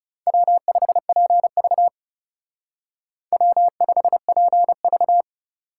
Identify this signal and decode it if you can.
Morse code, heard as a keyed tone, about 38 wpm, 700 Hz. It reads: W5PV W5PV